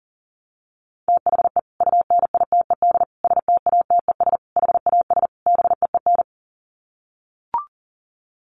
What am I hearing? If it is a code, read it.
Morse code, heard as a keyed tone, about 40 wpm, 700 Hz. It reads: THE UNITED STATES HAS BEEN
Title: Sets of 5 Words Encore2 40wpm